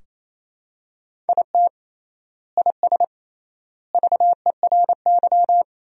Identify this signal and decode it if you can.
Morse code, heard as a keyed tone, about 28 wpm, 700 Hz. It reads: IT IS VERY
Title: 3x Word-Spacing - Sentences from Top 200 Words - 28wpm